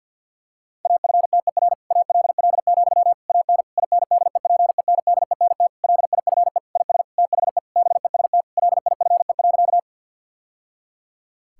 Morse code, heard as a keyed tone, about 50 wpm, 700 Hz. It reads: AFTER ALL, AN INDEPENDENT LIFE IS THE BEST LIFE.
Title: Sentences from Top 1000 Words - Review Part 6 - 50wpm